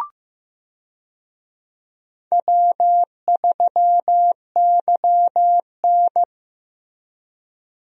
Morse code, heard as a keyed tone, about 15 wpm, 700 Hz. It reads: W3YN
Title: USA Call Signs - 4 Characters - Repeated 3x - 15wpm